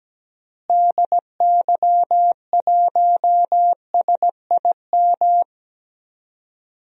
Morse code, heard as a keyed tone, about 17 wpm, 700 Hz. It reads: DY1SIM
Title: DXCC-100 Call Signs - 6 Characters - Repeated 3x - 17wpm